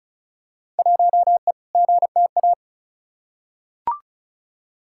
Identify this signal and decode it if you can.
Morse code, heard as a keyed tone, about 34 wpm, 700 Hz. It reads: 1E GTA
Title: Field Day Exchanges - 35wpm